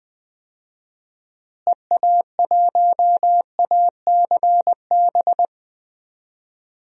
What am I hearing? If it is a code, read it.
Morse code, heard as a keyed tone, about 20 wpm, 700 Hz. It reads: EA1ACB
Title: DXCC-100 Call Signs - 6 Characters - Repeated 3x - 20wpm